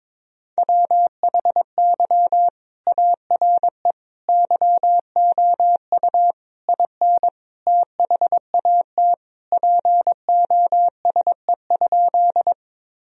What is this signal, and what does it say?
Morse code, heard as a keyed tone, about 22 wpm, 700 Hz. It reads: WHY ARE YOU IN THAT POSE?
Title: Sentences from Top 1000 Words - Review Part 2 - 22wpm